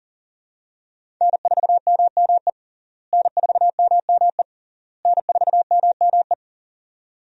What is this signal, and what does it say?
Morse code, heard as a keyed tone, about 37 wpm, 700 Hz. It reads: N4MME N4MME N4MME
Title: USA Call Signs - 5 Characters - Repeated 3x - 40wpm